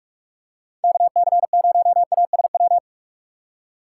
Morse code, heard as a keyed tone, about 46 wpm, 700 Hz. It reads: KC0ASW
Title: USA Call Signs - 6 Characters - 6x Character Spacing - 45wpm